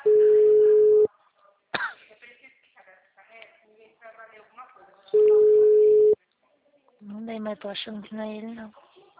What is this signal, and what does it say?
Ringback tone (Europe, 425 Hz), with speech.
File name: out- -2005- - - .62